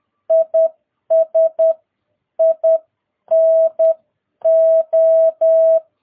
Morse code, heard as a keyed tone, about 10 wpm, spaced as for 9 wpm, 650 Hz. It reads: IS INO
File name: jh8cbh / 2022cw